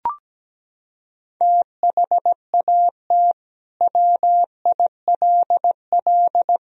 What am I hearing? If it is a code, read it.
Morse code, heard as a keyed tone, about 17 wpm, 700 Hz. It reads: THAT WILL
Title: Sentences from Top 100 Words and Abbreviations 17wpm